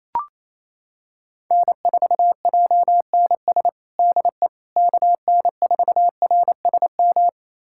Morse code, heard as a keyed tone, about 28 wpm, 700 Hz. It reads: N4JNS DE KN4RSM
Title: USA Callsign DE Callsign 28wpm